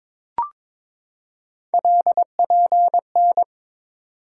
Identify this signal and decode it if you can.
Morse code, heard as a keyed tone, about 22 wpm, 700 Hz. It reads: LPN
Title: Practice QSO - Exchange 3 - More Info - 22wpm